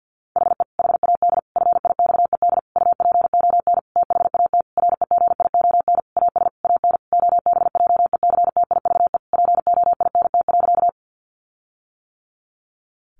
Morse code, heard as a keyed tone, about 50 wpm, 700 Hz. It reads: HE HAD FIXED UPON THAT REGION AS AN OBJECTIVE POINT.
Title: Sentences from Top 1500 Words 50wpm